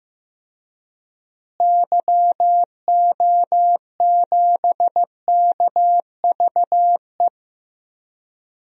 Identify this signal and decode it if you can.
Morse code, heard as a keyed tone, about 15 wpm, 700 Hz. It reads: YO7KVE
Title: DXCC-100 Call Signs - 6 Characters - Repeated 3x - 15wpm